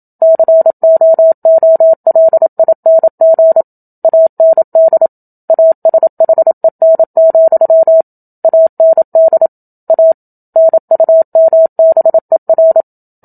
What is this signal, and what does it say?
Morse code, heard as a keyed tone, about 27 wpm, 650 Hz. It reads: COOLING AND ASHEN, AND A NUMBER
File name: Book Two / 01-Chapter 1 - 27 WPM